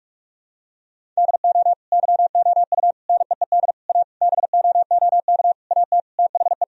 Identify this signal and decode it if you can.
Morse code, heard as a keyed tone, about 46 wpm, 700 Hz. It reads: DO YOU NEED A BOOK AT THE
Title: Sentences from Top 1000 Words - Review Part 4 - 45wpm